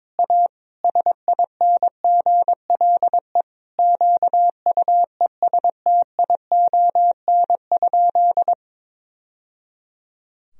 Morse code, heard as a keyed tone, about 22 wpm, 700 Hz. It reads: A SINGLE QUESTION?
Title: Sentences from Top 1000 Words - Review Part 4 - 22wpm